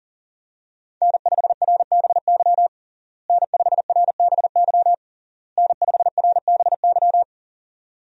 Morse code, heard as a keyed tone, about 40 wpm, 700 Hz. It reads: N5RBY N5RBY N5RBY
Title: USA Call Signs - 5 Characters - Repeated 3x - 40wpm